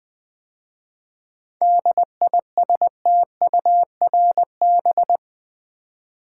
Morse code, heard as a keyed tone, about 20 wpm, 700 Hz. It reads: DISTURB
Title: 7 Letter Words - First 3 Letters Given - 20wpm